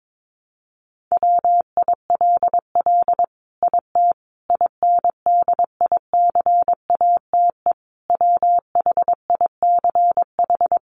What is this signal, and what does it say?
Morse code, heard as a keyed tone, about 22 wpm, 700 Hz. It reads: WILL IT INDICATE WHICH
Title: Sentences from Top 1300 Words 22wpm